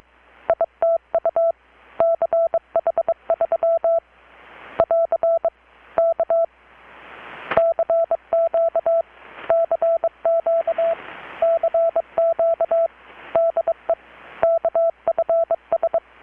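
Morse code, heard as a keyed tone, about 23 wpm, 660 Hz. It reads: ITU CH3 <AR> K CQ CQ CQ DE KFS
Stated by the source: Morse from KPH's Night of Nights XX maritime radiotelegraphy event, received in Chiapas, Mexico about 3,800 km away on a Kenwood R-600 with a grounded 10 m long-wire antenna.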